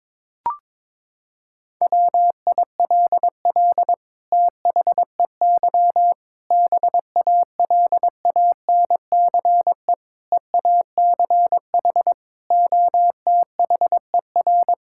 Morse code, heard as a keyed tone, about 22 wpm, 700 Hz. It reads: WILL THEY BALANCE EACH OTHER
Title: Sentences from Top 1000 Words - Review Part 6 - 22wpm